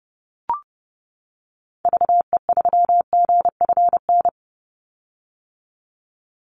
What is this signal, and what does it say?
Morse code, heard as a keyed tone, about 30 wpm, 700 Hz. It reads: VE3GFN
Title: Call Signs - 30wpm